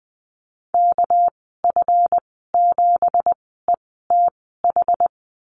Morse code, heard as a keyed tone, about 20 wpm, spaced as for 15 wpm, 700 Hz. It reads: KF7ETH
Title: 231 Lesson 37g - Letter x - Call Signs 20wpm